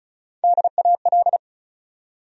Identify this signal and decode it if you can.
Morse code, heard as a keyed tone, about 36 wpm, 700 Hz. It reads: DAL
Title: USA Call Sign Postfixes - 3 Letters 35wpm